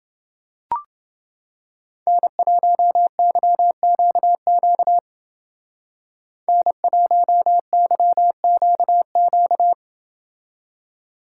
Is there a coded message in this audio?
Yes: Morse code from 2.1 to 9.7 s, heard as a keyed tone, about 30 wpm, 700 Hz. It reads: N1YQQ N1YQQ